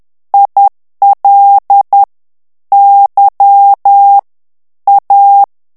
Morse code, heard as a keyed tone, about 10 wpm, 800 Hz. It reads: IL Y A
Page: Français - texte en clair 1 - vitesse : 10 mots minute